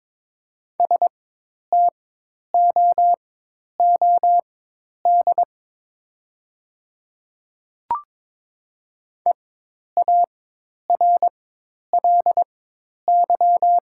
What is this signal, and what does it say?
Morse code, heard as a keyed tone, about 21 wpm, spaced as for 10 wpm, 700 Hz. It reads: STOOD EARLY
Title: 4x Character Spacing - Top 500 Words - 22wpm